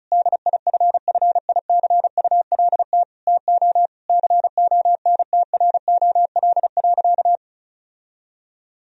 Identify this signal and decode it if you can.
Morse code, heard as a keyed tone, about 36 wpm, 700 Hz. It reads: DIFFICULT TO CONTROL.
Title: Sentences from Top 1000 Words - Review Part 4 - 35wpm